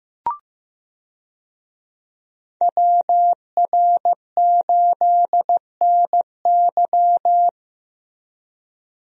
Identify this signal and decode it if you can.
Morse code, heard as a keyed tone, about 15 wpm, 700 Hz. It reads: WR8NY
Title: USA Call Signs - 5 Characters - Repeated 2x - 15wpm